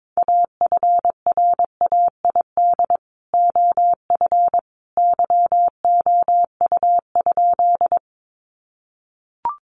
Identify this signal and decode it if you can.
Morse code, heard as a keyed tone, about 22 wpm, 700 Hz. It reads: AFRAID OF YOU?
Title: Sentences from Top 1500 Words 22wpm